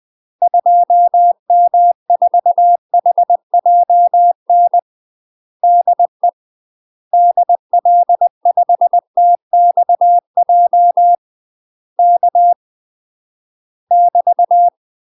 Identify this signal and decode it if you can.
Morse code, heard as a keyed tone, about 20 wpm, 700 Hz. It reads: 2M4HJN DE DL5TXJ K <BT>
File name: UK MASTER QSOs TRAINING/20WPM